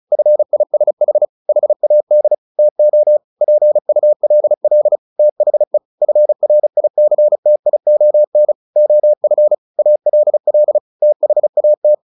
Morse code, heard as a keyed tone, about 34 wpm, 600 Hz. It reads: FISH HAD TO PULL THE FRICTION OF ALL THAT
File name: the old man and the sea 2 35wpm